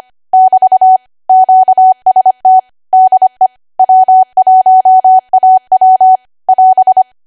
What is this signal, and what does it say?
Morse code, heard as a keyed tone, about 25 wpm, 750 Hz. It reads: <BT> QST DE W1AW <AS>
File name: Archive / 25 WPM / 25WPM